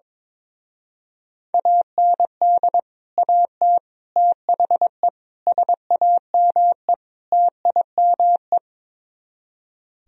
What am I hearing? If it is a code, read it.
Morse code, heard as a keyed tone, about 22 wpm, 700 Hz. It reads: AND AT THE SAME TIME
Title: Speed-Racing - Sets of 5 Words 22wpm